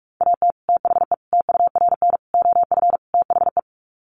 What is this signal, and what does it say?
Morse code, heard as a keyed tone, about 44 wpm, 700 Hz. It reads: AT THE TURN OF THE
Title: Sets of 5 Words 45wpm